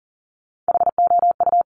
Morse code, heard as a keyed tone, about 39 wpm, 690 Hz. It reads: HOU
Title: Top Three Letter Combinations in Words 40wpm